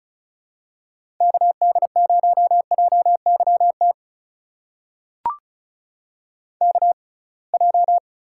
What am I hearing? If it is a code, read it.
Morse code, heard as a keyed tone, about 34 wpm, spaced as for 26 wpm, 700 Hz. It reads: KD0JYT KJ
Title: USA Call Signs - 6 Characters - 6x Character Spacing - 35wpm